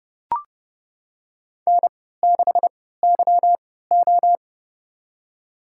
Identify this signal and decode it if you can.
Morse code, heard as a keyed tone, about 31 wpm, spaced as for 17 wpm, 700 Hz. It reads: N6YO
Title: USA Call Signs - 4 Characters - 3x Character Spacing - 30wpm